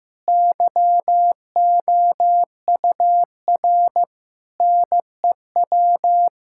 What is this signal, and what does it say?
Morse code, heard as a keyed tone, about 15 wpm, 700 Hz. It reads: YOUR NEW
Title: Sentences from Top 1000 Words - Review Part 5 - 15wpm